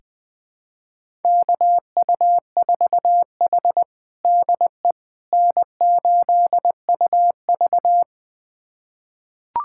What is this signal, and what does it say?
Morse code, heard as a keyed tone, about 21 wpm, 700 Hz. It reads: KU4H DE N8UV
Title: USA Callsign DE Callsign - 4 Characters - 20wpm